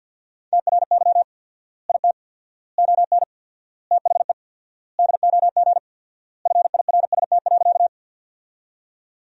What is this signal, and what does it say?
Morse code, heard as a keyed tone, about 50 wpm, 700 Hz. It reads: TRY IT ON THE DOG FIRST.
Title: 4x Word-Spacing - Sentences from Top 400 Words - 50wpm